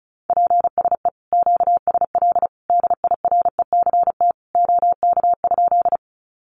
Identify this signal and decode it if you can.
Morse code, heard as a keyed tone, about 34 wpm, 700 Hz. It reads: PSE QSL DIRECT OK?
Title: Common QSO Phrases 35wpm